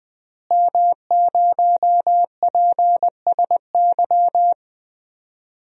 Morse code, heard as a keyed tone, about 21 wpm, 700 Hz. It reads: M0PSY
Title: DXCC-100 Call Signs - 5 Characters - Repeated 2x - 20wpm